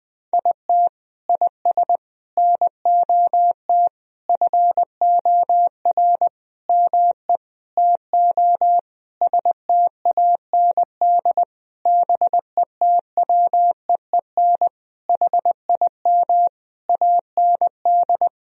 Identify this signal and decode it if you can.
Morse code, heard as a keyed tone, about 20 wpm, 700 Hz. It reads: IT IS NOT FOR ME TO STAND BETWEEN HIM AND
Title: Sentences from Top 1000 Words - Review Part 1 - 20wpm